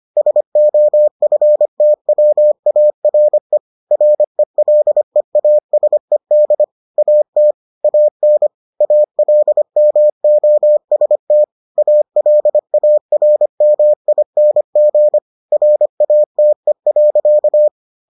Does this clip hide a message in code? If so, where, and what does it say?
Yes: Morse code from 0.2 to 17.7 s, heard as a keyed tone, about 25 wpm, 600 Hz. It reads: SOFTWARE RELEASED AT AN ALMOST ALARMING RATE.